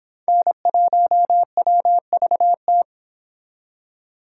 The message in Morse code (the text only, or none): N1WVT